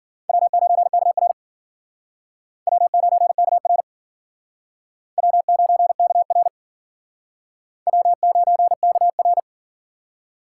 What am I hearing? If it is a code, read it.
Morse code, heard as a keyed tone, about 51 wpm, 700 Hz. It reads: W9KR W9KR W9KR W9KR